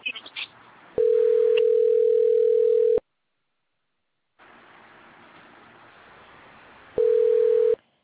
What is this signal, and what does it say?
Ringback tone (North America), 440 + 480 Hz.